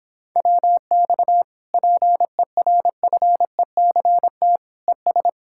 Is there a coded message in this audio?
Yes: Morse code from 0.4 to 5.3 s, heard as a keyed tone, about 29 wpm, 700 Hz. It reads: WX PERFECT ES